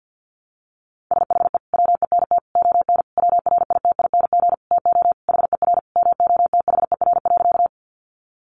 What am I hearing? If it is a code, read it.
Morse code, heard as a keyed tone, about 50 wpm, 700 Hz. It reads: SHE WENT ON WRITING TO HER MOTHER.
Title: Sentences from Top 800 Words 50wpm